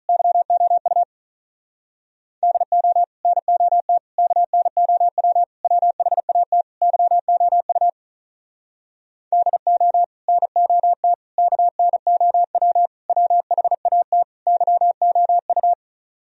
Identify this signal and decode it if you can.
Morse code, heard as a keyed tone, about 44 wpm, 700 Hz. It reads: YOU DO NOT KNOW WHAT YOU DO NOT KNOW WHAT YOU
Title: Speed-Racing - Sets of 5 Words 35wpm